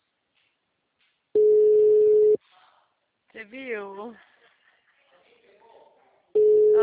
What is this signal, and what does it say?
Ringback tone (Europe, 425 Hz), with speech.